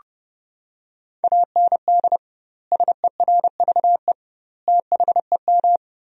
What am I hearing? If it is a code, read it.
Morse code, heard as a keyed tone, about 32 wpm, 700 Hz. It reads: AND SERVE THEM
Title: 2x Word-Spacing - Sentences from Top 400 Words - 30wpm